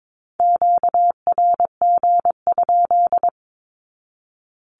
Morse code, heard as a keyed tone, about 22 wpm, 700 Hz. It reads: QRG?
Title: Q-Codes 22wpm